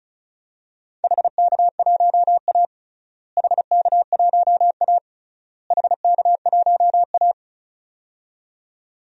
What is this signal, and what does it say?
Morse code, heard as a keyed tone, about 34 wpm, 700 Hz. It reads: HK1A HK1A HK1A
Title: DXCC-100 Call Signs - 4 Characters - Repeated 3x - 35wpm